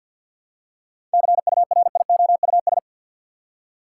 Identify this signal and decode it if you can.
Morse code, heard as a keyed tone, about 47 wpm, 700 Hz. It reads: CURIOUS
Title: Seven Letter Words - 50wpm